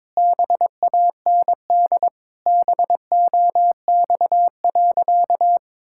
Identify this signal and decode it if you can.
Morse code, heard as a keyed tone, about 22 wpm, 700 Hz. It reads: BAND BOX.
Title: Sentences from Top 1600 Words 22wpm